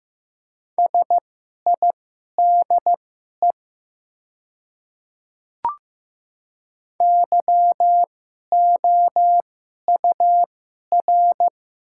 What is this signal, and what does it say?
Morse code, heard as a keyed tone, about 15 wpm, spaced as for 11 wpm, 700 Hz. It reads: SIDE YOUR
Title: 2x Character Spacing - Top 200 Words - 15wpm